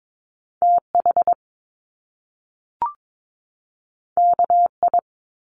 Morse code, heard as a keyed tone, about 22 wpm, 700 Hz. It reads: TH KI
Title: Top Two Letter Combinations in Words 22wpm